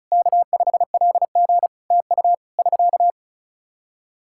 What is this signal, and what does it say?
Morse code, heard as a keyed tone, about 36 wpm, 700 Hz. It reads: K5LG TU <SK>
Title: K1USN SST QSO Practice 35wpm